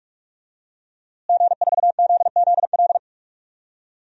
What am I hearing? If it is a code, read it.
Morse code, heard as a keyed tone, about 45 wpm, 700 Hz. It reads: G4ZZL
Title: DXCC-100 Call Signs - 5 Characters - Repeated 3x - 45wpm